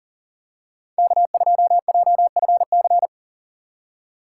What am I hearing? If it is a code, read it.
Morse code, heard as a keyed tone, about 37 wpm, 700 Hz. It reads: K2JFC
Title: USA Call Signs - 5 Characters - 3x Character Spacing - 40wpm